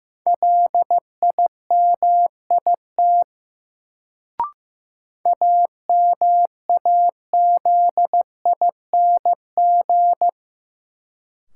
Morse code, heard as a keyed tone, about 15 wpm, 700 Hz. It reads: LIMIT AMAZING
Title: Sentences from Top 1500 Words - New Word Drill - 15wpm